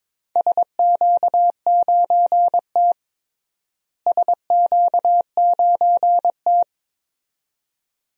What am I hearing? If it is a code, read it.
Morse code, heard as a keyed tone, about 22 wpm, 700 Hz. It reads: SQ9T SQ9T
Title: DXCC-100 Call Signs - 4 Characters - Repeated 2x - 22wpm